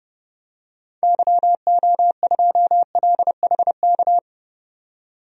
Morse code, heard as a keyed tone, about 30 wpm, 700 Hz. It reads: YO2LHK
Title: DXCC-100 Call Signs - 6 Characters - 30wpm